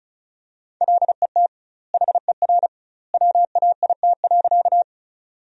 Morse code, heard as a keyed tone, about 34 wpm, 700 Hz. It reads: LET HER WAIT.
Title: 2x Word-Spacing - Sentences from Top 500 Words - 35wpm